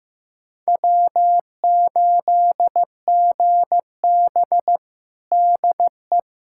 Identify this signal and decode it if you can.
Morse code, heard as a keyed tone, about 15 wpm, 700 Hz. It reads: W8GB DE
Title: USA Callsign DE Callsign - 4 Characters - 15wpm